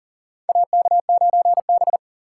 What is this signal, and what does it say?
Morse code, heard as a keyed tone, about 40 wpm, 700 Hz. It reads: AK9B
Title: USA Call Signs - 4 Characters - 3x Character Spacing - 40wpm